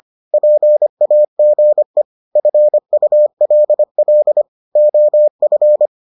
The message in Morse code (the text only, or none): PAGE FULL OF